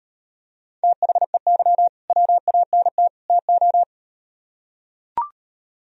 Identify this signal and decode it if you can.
Morse code, heard as a keyed tone, about 38 wpm, 700 Hz. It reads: THEY WANT TO